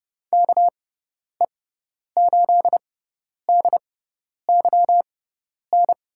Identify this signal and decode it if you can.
Morse code, heard as a keyed tone, about 30 wpm, spaced as for 10 wpm, 700 Hz. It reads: KE8DYN
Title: USA Call Signs - 6 Characters - 6x Character Spacing - 30wpm